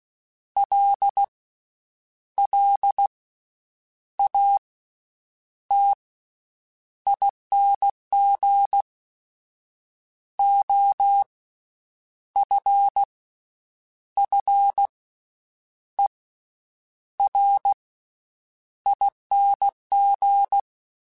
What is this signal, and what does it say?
Morse code, heard as a keyed tone, about 16 wpm, spaced as for 8 wpm, 800 Hz. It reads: LLATING OFFERING
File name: Special INGs / Special INGs-mp3s / VerySlowToFaster Ing @15